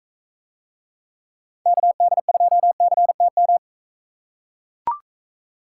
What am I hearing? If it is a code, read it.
Morse code, heard as a keyed tone, about 42 wpm, 700 Hz. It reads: KD2CTM